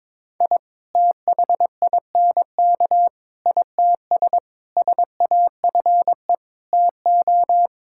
Morse code, heard as a keyed tone, about 22 wpm, 700 Hz. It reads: I THINK ITS SAFE TO
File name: practice / 22wpm / Sets of 6 Words - 22wpm